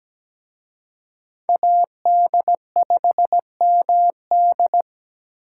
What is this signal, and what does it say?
Morse code, heard as a keyed tone, about 20 wpm, 700 Hz. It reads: AD5MD
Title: Speed Racing - USA 2x2 Call Signs 17wpm